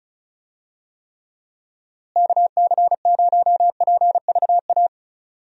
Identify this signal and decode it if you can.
Morse code, heard as a keyed tone, about 36 wpm, 700 Hz. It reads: KC0PVA
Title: USA Call Signs - 6 Characters - Repeated 2x - 35wpm